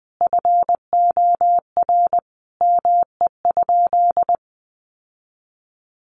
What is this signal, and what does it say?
Morse code, heard as a keyed tone, about 20 wpm, 700 Hz. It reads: FOR ME?
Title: Sentences from Top 1500 Words 20wpm